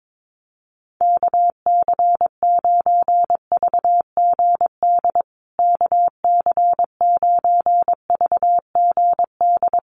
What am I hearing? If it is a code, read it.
Morse code, heard as a keyed tone, about 22 wpm, 700 Hz. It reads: KC9VGD KC9VGD
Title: USA Call Signs Repeated 3x 22wpm